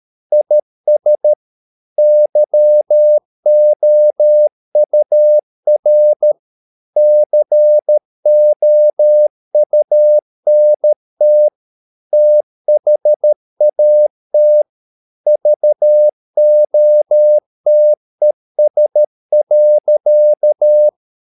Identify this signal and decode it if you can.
Morse code, heard as a keyed tone, about 13 wpm, 600 Hz. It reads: IS YOUR COUNT THAT VOTES.
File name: mp3 files / Lexiphiles - 13wpm